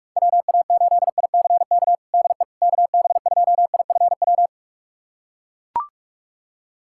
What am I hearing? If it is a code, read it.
Morse code, heard as a keyed tone, about 41 wpm, 690 Hz. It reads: WA8ICK DE KB2IFW
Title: USA Callsign DE Callsign 45wpm